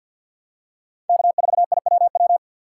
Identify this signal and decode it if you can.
Morse code, heard as a keyed tone, about 48 wpm, 700 Hz. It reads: K4IWW
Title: USA Call Signs - 5 Characters - 3x Character Spacing - 50wpm